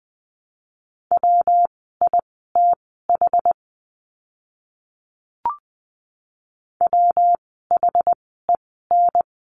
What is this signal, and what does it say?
Morse code, heard as a keyed tone, about 20 wpm, spaced as for 14 wpm, 700 Hz. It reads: WITH WHEN
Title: 192 Lesson 31f - Letter v - Words 20wpm